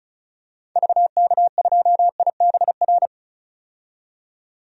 Morse code, heard as a keyed tone, about 36 wpm, 700 Hz. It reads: VK2IBR
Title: DXCC-100 Call Signs - 6 Characters - 35wpm